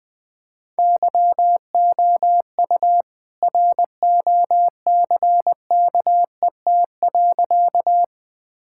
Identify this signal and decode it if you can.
Morse code, heard as a keyed tone, about 20 wpm, 700 Hz. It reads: YOU ROCKET.
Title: All-Age Appropriate Jokes - 20wpm